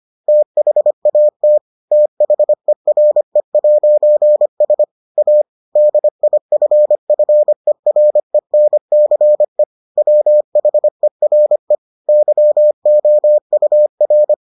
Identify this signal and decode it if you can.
Morse code, heard as a keyed tone, about 25 wpm, 600 Hz. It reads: THAT THERE'S A DIFFERENCE WHERE YOUR